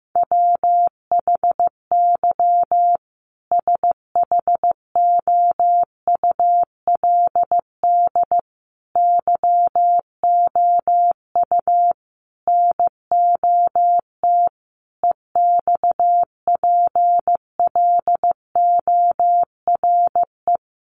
Morse code, heard as a keyed tone, about 15 wpm, 700 Hz. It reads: WHY SHOULD YOU NOT EXPLORE
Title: Sentences from Top 1100 Words 15wpm